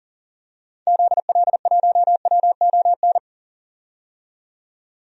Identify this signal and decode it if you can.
Morse code, heard as a keyed tone, about 40 wpm, 700 Hz. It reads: ZL1WON
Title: DXCC-100 Call Signs - 6 Characters - 40wpm